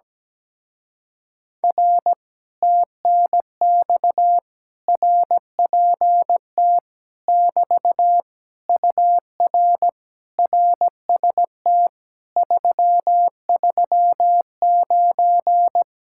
Morse code, heard as a keyed tone, about 17 wpm, 700 Hz. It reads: R TNX RPT <BT> UR RST 339
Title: Speed-Racing Common QSO Phrases - 17wpm